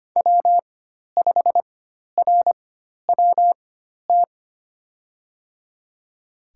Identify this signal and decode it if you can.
Morse code, heard as a keyed tone, about 25 wpm, spaced as for 12 wpm, 700 Hz. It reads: W5RWT
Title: USA Call Signs - 5 Characters - 4x Character Spacing - 25wpm